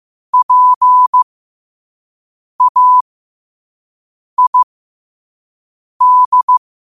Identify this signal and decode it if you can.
Morse code, heard as a keyed tone, about 15 wpm, spaced as for 5 wpm, 1000 Hz. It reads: PAID
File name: qotdfiles / 5wpm